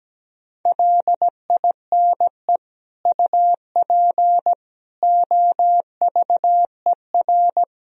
Morse code, heard as a keyed tone, about 17 wpm, 700 Hz. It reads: LINE UP OVER
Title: Sentences from Top 1000 Words - Review Part 1 - 17wpm